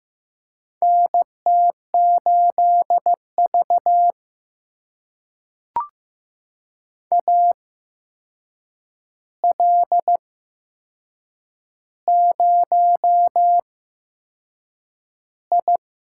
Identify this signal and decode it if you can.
Morse code, heard as a keyed tone, about 15 wpm, spaced as for 6 wpm, 700 Hz. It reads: NT8V AL0I
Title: USA Call Signs - 4 Characters - 8x Character Spacing - 15wpm